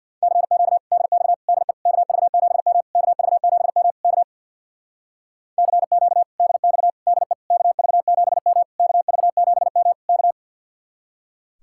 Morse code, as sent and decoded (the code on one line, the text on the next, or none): -.-. --.- / -.. -..- / -.. / -.- ...- --... -- / -.- ...- --... -- / -.- / -.-. --.- / -.. -..- / -.. / -.- ...- --... -- / -.- ...- --... -- / -.-
CQ DX DE KV7M KV7M K CQ DX DE KV7M KV7M K